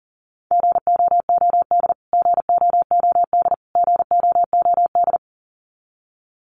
Morse code, heard as a keyed tone, about 40 wpm, 700 Hz. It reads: GOOD GOOD GOOD
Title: Top 500 Words Repeated 40wpm